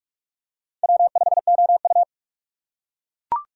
Morse code, heard as a keyed tone, about 41 wpm, 690 Hz. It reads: W5OU
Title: USA Call Signs - 4 Characters - 3x Character Spacing - 45wpm